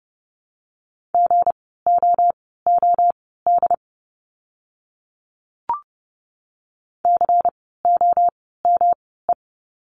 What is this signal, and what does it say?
Morse code, heard as a keyed tone, about 30 wpm, spaced as for 17 wpm, 700 Hz. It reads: GOOD COME
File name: course / 30:10wpm / 237 Lesson 38f - Letter q - Words 30wpm